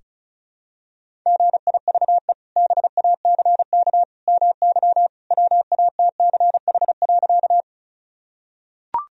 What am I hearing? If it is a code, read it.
Morse code, heard as a keyed tone, about 36 wpm, 700 Hz. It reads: GIVE BACK MY WATCH.
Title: Sentences from Top 1000 Words - Review Part 1 - 35wpm